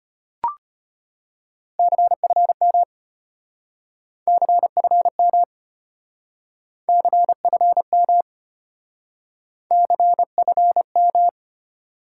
Morse code, heard as a keyed tone, about 32 wpm, 700 Hz. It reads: CFM CFM CFM CFM